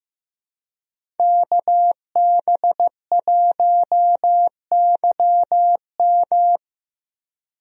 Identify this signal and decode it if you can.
Morse code, heard as a keyed tone, about 15 wpm, 700 Hz. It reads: KB1YM
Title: USA Call Signs - 5 Characters - Repeated 2x - 15wpm